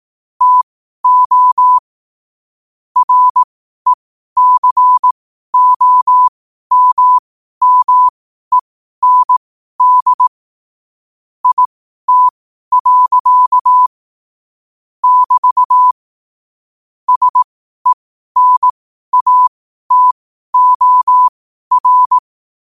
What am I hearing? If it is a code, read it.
Morse code, heard as a keyed tone, about 18 wpm, spaced as for 12 wpm, 1000 Hz. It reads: TO RECOMMEND IT. <BT> SENATOR
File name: qotdfiles / 12wpm